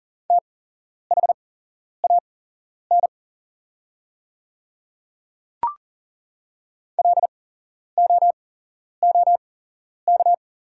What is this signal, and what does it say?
Morse code, heard as a keyed tone, about 39 wpm, spaced as for 11 wpm, 700 Hz. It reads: THAN LOOK
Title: 8x Character Spacing - Top 100 Words - 40wpm